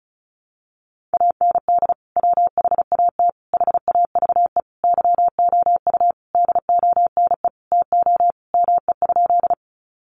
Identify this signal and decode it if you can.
Morse code, heard as a keyed tone, about 34 wpm, 700 Hz. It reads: AND WHAT HAVE YOU DONE TO ME?
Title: Sentences from Top 500 Words 35wpm